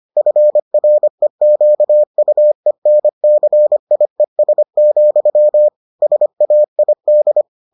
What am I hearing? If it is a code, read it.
Morse code, heard as a keyed tone, about 25 wpm, 600 Hz. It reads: FREQUENCIES, SAID